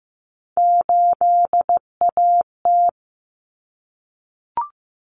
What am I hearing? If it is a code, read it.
Morse code, heard as a keyed tone, about 15 wpm, 700 Hz. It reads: 8AT
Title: Common Cut Numbers - 3-Digit 15wpm